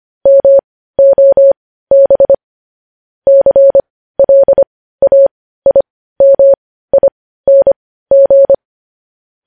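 Morse code, heard as a keyed tone, about 25 wpm, spaced as for 15 wpm, 550 Hz. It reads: MOB CLAIMING
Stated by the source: Morse code news headlines at 15.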